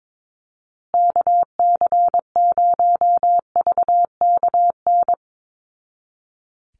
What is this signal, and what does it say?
Morse code, heard as a keyed tone, about 22 wpm, 700 Hz. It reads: KC0VKN
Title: Call Signs Repeated 2x - 22wpm